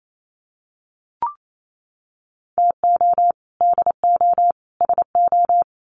Morse code, heard as a keyed tone, about 28 wpm, 700 Hz. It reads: TO DO SO
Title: Sets of 3 Words 28wpm